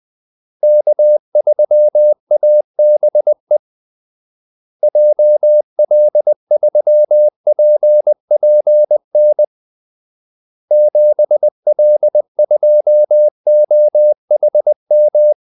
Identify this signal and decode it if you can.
Morse code, heard as a keyed tone, about 20 wpm, 600 Hz. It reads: K3ABE JL3PPN 7L2OHM
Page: Kaldesignaler 20 wpm